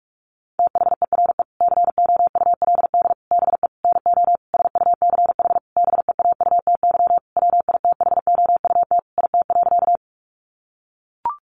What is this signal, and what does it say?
Morse code, heard as a keyed tone, about 44 wpm, 700 Hz. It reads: THERE COULD BE NO SUCH BEAUTY WITHOUT IT.
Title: Sentences from Top 600 Words 45wpm